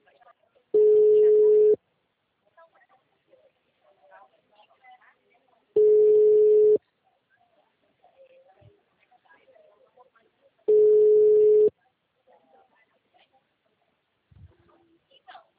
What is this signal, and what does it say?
Ringback tone (Europe, 425 Hz), with speech.
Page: Rio 3 - Ura